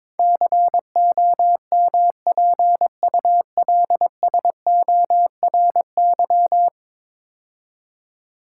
Morse code, heard as a keyed tone, about 22 wpm, 700 Hz. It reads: COMPULSORY
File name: practice / 22wpm / 10 Letter Words - First 3 Letters Given - 22wpm